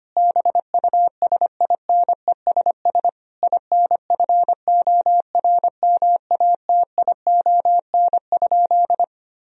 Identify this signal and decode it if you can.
Morse code, heard as a keyed tone, about 25 wpm, 700 Hz. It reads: BUSINESS INFORMATION?
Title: Sentences from Top 1000 Words - Review Part 3 - 25wpm